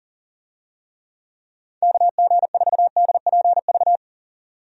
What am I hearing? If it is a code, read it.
Morse code, heard as a keyed tone, about 40 wpm, 700 Hz. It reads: KG4DPV
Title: USA Call Signs - 6 Characters - 40wpm